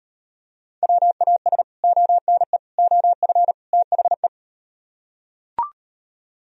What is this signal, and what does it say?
Morse code, heard as a keyed tone, about 39 wpm, 700 Hz. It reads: WAS ONE OF THE